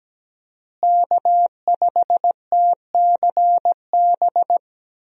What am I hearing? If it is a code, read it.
Morse code, heard as a keyed tone, about 17 wpm, 700 Hz. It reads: K5TCB
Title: USA Call Signs - 5 Characters - Repeated 2x - 17wpm